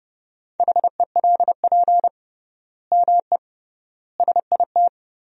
Morse code, heard as a keyed tone, about 32 wpm, 700 Hz. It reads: HELP ME SIT
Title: 3x Word-Spacing - Sentences from Top 500 Words - 30wpm